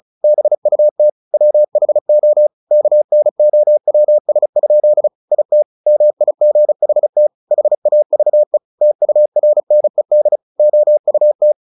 Morse code, heard as a keyed tone, about 34 wpm, 600 Hz. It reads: BUT WHO KNOWS? IT MIGHT HAVE TURNED OUT
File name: the old man and the sea 2 35wpm